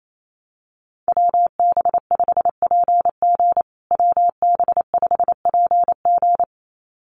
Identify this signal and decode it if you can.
Morse code, heard as a keyed tone, about 28 wpm, 700 Hz. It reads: WB5PG WB5PG
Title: USA Call Signs Repeated 2x 28wpm